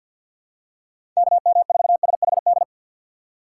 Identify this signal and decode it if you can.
Morse code, heard as a keyed tone, about 52 wpm, 700 Hz. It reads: KM4SHD
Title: USA Call Signs - 6 Characters - 6x Character Spacing - 50wpm